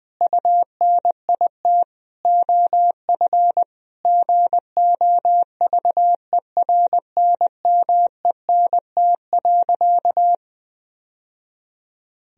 Morse code, heard as a keyed tone, about 20 wpm, 700 Hz. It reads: UNIT OF GOVERNMENT.
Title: Sentences from Top 1000 Words - Review Part 2 - 20wpm